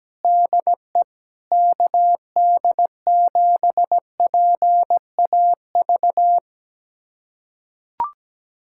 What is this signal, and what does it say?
Morse code, heard as a keyed tone, about 17 wpm, 700 Hz. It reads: DE KD7PAV
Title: USA Callsign DE Callsign 17wpm